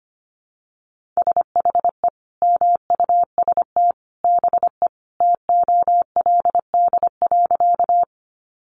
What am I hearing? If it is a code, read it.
Morse code, heard as a keyed tone, about 25 wpm, 700 Hz. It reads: SHE MUST BE TOLD.
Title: Sentences from Top 400 Words 25wpm